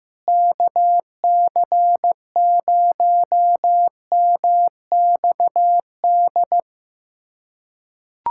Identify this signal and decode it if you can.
Morse code, heard as a keyed tone, about 15 wpm, 700 Hz. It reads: KC0MXD
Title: USA Callsign DE Callsign - 6 Characters - 15wpm